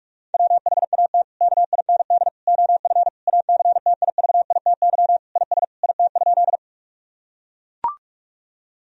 Morse code, heard as a keyed tone, about 41 wpm, 700 Hz. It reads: WHAT KIND OF ACTIVITY IS IT?
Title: Sentences from Top 1000 Words - Review Part 4 - 45wpm